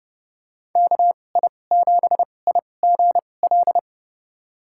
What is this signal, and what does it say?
Morse code, heard as a keyed tone, about 32 wpm, spaced as for 22 wpm, 700 Hz. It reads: KI7IGL
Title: USA Call Signs - 2x Character Spacing - 30wpm